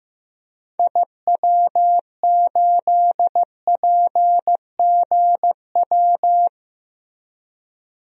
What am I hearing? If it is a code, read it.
Morse code, heard as a keyed tone, about 16 wpm, 700 Hz. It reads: IW8PGW
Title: DXCC-100 Call Signs - 6 Characters - Repeated 2x - 15wpm